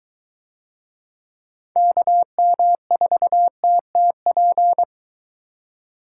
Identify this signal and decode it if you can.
Morse code, heard as a keyed tone, about 21 wpm, 700 Hz. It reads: KM4TTP